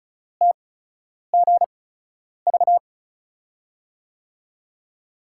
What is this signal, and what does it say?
Morse code, heard as a keyed tone, about 36 wpm, spaced as for 10 wpm, 700 Hz. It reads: TGV
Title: USA Call Signs - 5 Characters - 8x Character Spacing - 35wpm